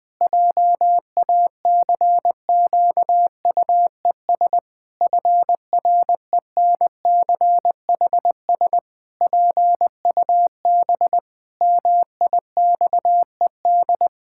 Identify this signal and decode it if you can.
Morse code, heard as a keyed tone, about 20 wpm, 700 Hz. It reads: JACQUES FRENCHS PUB MIXED
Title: Pangram Challenge v2 - 20wpm